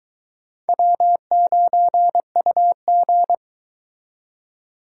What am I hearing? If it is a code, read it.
Morse code, heard as a keyed tone, about 22 wpm, 700 Hz. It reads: W9UG